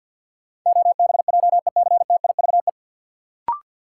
Morse code, heard as a keyed tone, about 52 wpm, 700 Hz. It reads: OBJECTIVE
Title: Sentences from Top 1200 Words - New Word Drill - 50wpm